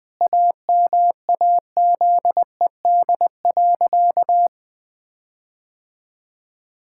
Morse code, heard as a keyed tone, about 20 wpm, 700 Hz. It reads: AMAZED.
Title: Sentences with 4-Word Phrases - Spoken Beginning - 20wpm